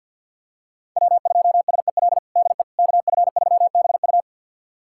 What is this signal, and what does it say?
Morse code, heard as a keyed tone, about 52 wpm, 690 Hz. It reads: W2SEL DE KF3BU
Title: USA Callsign DE Callsign - 5 Characters - 50wpm